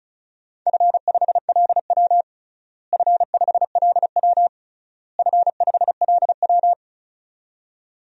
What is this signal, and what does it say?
Morse code, heard as a keyed tone, about 34 wpm, 700 Hz. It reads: F5LW F5LW F5LW
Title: DXCC-100 Call Signs - 4 Characters - Repeated 3x - 35wpm